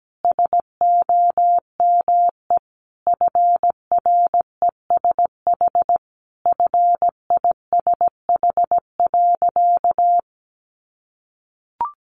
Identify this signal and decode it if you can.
Morse code, heard as a keyed tone, about 17 wpm, 700 Hz. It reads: SOME FRESH FISH.
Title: Sentences from Top 1500 Words 17wpm